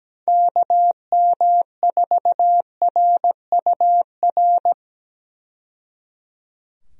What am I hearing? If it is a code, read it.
Morse code, heard as a keyed tone, about 17 wpm, 700 Hz. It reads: KM4RUR
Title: USA Call Signs - 6 Characters - 17wpm